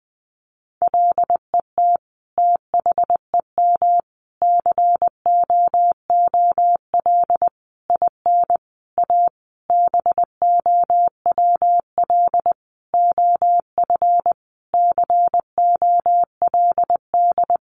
Morse code, heard as a keyed tone, about 20 wpm, 700 Hz. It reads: LET THEM COOL IN A BOWL OF COLD
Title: Sentences from Top 1200 Words 20wpm